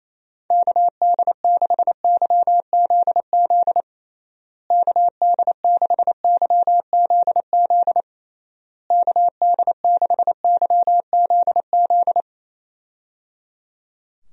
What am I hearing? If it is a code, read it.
Morse code, heard as a keyed tone, about 28 wpm, 700 Hz. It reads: KD6YZZ KD6YZZ KD6YZZ
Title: USA Call Signs - 6 Characters - Repeated 3x - 28wpm